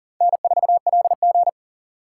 Morse code, heard as a keyed tone, about 40 wpm, 700 Hz. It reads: N4LG
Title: USA Call Signs - 2x Character Spacing - 40wpm